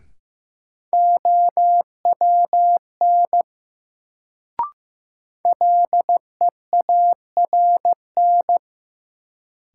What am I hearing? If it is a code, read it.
Morse code, heard as a keyed tone, about 15 wpm, 700 Hz. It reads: OWN LEARN
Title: Sentences from Top 300 Words - New Word Drill - 15wpm